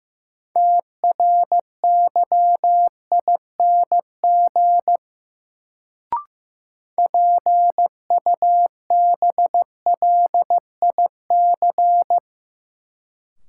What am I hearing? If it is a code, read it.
Morse code, heard as a keyed tone, about 15 wpm, 700 Hz. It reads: TRYING PUBLIC
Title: Sets of 2 Words - New Word Drill - 15wpm